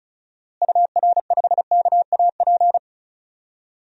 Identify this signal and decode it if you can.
Morse code, heard as a keyed tone, about 34 wpm, 700 Hz. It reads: UR5KAP
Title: DXCC-100 Call Signs - 6 Characters - Repeated 3x - 35wpm